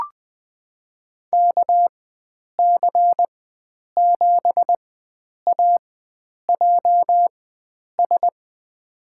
Morse code, heard as a keyed tone, about 20 wpm, spaced as for 9 wpm, 700 Hz. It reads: KC7AJS
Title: USA Call Signs - 6 Characters - 4x Character Spacing - 20wpm